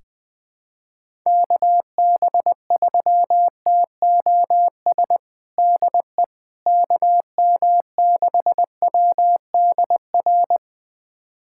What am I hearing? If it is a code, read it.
Morse code, heard as a keyed tone, about 20 wpm, 700 Hz. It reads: KB3TOS DE KM6WDR
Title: USA Callsign DE Callsign - 6 Characters - 20wpm